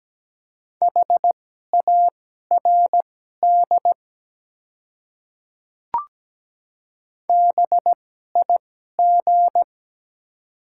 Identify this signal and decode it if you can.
Morse code, heard as a keyed tone, about 17 wpm, spaced as for 12 wpm, 690 Hz. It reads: HARD BIG
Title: 2x Character Spacing - Top 300 Words - 17wpm